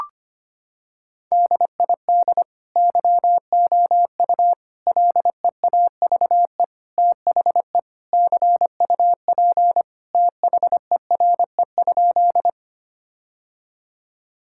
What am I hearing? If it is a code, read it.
Morse code, heard as a keyed tone, about 25 wpm, 700 Hz. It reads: DID YOU LEAVE THE CUP THERE?
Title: Sentences from Top 1600 Words 25wpm